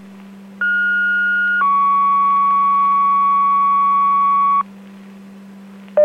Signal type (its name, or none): two-tone page